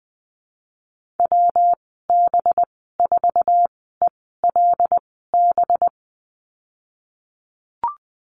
Morse code, heard as a keyed tone, about 20 wpm, spaced as for 15 wpm, 700 Hz. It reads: WB4ELB
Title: 186 Lesson 30g - Letter b - Call Signs 20wpm